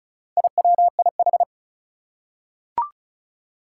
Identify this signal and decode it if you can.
Morse code, heard as a keyed tone, about 36 wpm, 700 Hz. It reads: IWIH
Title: Four Challenging Letter-Number Combinations 35wpm